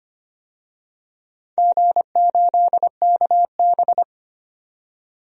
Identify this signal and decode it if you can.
Morse code, heard as a keyed tone, about 25 wpm, 700 Hz. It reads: G8KB
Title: DXCC-100 Call Signs - 4 Characters - 25wpm